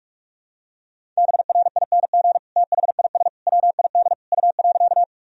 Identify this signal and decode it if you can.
Morse code, heard as a keyed tone, about 42 wpm, 700 Hz. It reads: BRING THIS WID U.
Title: Sentences from Top 500 Words and Abbreviations 45wpm